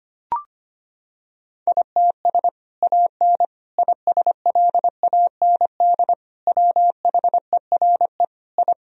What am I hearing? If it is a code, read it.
Morse code, heard as a keyed tone, about 25 wpm, 700 Hz. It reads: ITS AN ISLAND WHERE I
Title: Sentences from Top 1000 Words - Review Part 3 - 25wpm